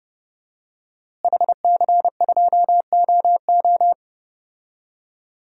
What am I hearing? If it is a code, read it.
Morse code, heard as a keyed tone, about 30 wpm, 700 Hz. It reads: HC2OO
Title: DXCC-100 Call Signs - 5 Characters - Repeated 2x - 30wpm